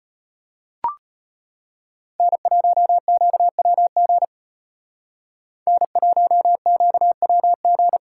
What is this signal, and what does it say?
Morse code, heard as a keyed tone, about 33 wpm, 700 Hz. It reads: N1QWG N1QWG